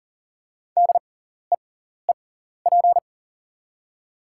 Morse code, heard as a keyed tone, about 40 wpm, spaced as for 14 wpm, 700 Hz. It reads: DEEP
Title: 6x Character Spacing - Top 500 Words - 40wpm